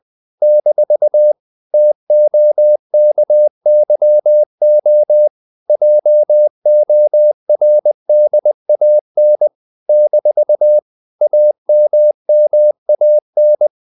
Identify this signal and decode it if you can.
Morse code, heard as a keyed tone, about 20 wpm, 600 Hz. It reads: - TOKYO JORDAN - AMMAN
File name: CW / Practice / NationalCapitals20